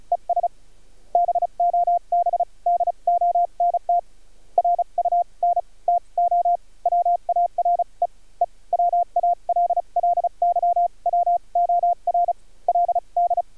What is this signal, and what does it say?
Morse code, heard as a keyed tone, about 36 wpm, spaced as for 28 wpm, 700 Hz. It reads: ES BOB DONT RUN TO WAREE WALLY WORLD
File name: archived / CRSnet